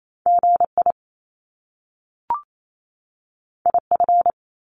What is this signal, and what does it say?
Morse code, heard as a keyed tone, about 28 wpm, 700 Hz. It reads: GI IF
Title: Top Two Letter Combinations in Words 28wpm